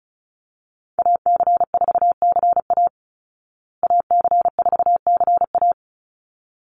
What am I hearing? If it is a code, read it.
Morse code, heard as a keyed tone, about 34 wpm, 700 Hz. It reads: AC4CA AC4CA
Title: Call Signs Repeated 3x - 35wpm